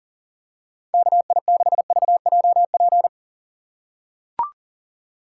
Morse code, heard as a keyed tone, about 38 wpm, 690 Hz. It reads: KI6VJP
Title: USA Call Signs - 4x Character Spacing - 40wpm